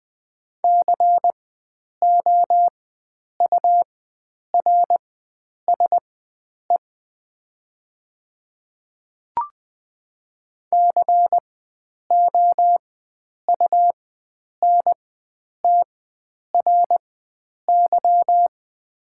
Morse code, heard as a keyed tone, about 20 wpm, spaced as for 9 wpm, 700 Hz. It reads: COURSE COUNTRY
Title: 4x Character Spacing - Top 500 Words - 20wpm